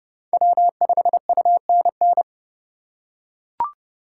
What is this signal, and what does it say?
Morse code, heard as a keyed tone, about 30 wpm, 700 Hz. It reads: W5UNN
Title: USA Call Signs - 5 Characters - 3x Character Spacing - 30wpm